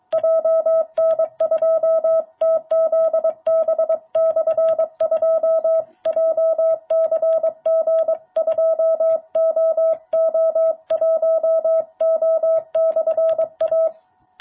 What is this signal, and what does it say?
Morse code, heard as a keyed tone, about 23 wpm, 660 Hz. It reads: JN2TZB/2 JCG2OO1O/A
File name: jh8cbh / 2021cw